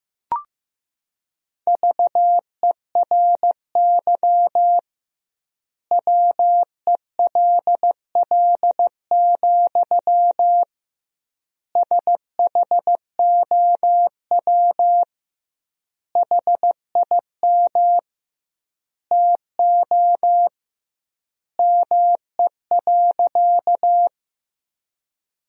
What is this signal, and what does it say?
Morse code, heard as a keyed tone, about 15 wpm, 700 Hz. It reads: VERY WELL, SHOW HIM TO ME.
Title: 2x Word-Spacing - Sentences from Top 200 Words - 15wpm